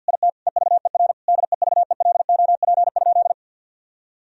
Morse code, heard as a keyed tone, about 46 wpm, 700 Hz. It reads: IT EVER DEVELOP?
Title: Sentences from Top 1000 Words - Review Part 2 - 50wpm